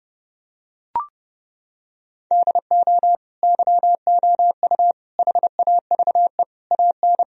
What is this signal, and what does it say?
Morse code, heard as a keyed tone, about 29 wpm, 700 Hz. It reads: DO YOU HAVE AN
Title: Sentences from Top 1000 Words - Review Part 3 - 30wpm